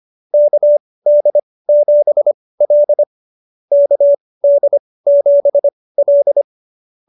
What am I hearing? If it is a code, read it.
Morse code, heard as a keyed tone, about 25 wpm, spaced as for 18 wpm, 590 Hz. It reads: KD7L KD7L